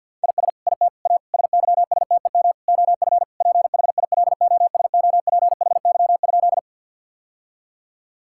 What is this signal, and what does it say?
Morse code, heard as a keyed tone, about 50 wpm, 700 Hz. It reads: IS IT A SYSTEM OF PHILOSOPHY?
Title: Sentences from Top 1000 Words - Review Part 6 - 50wpm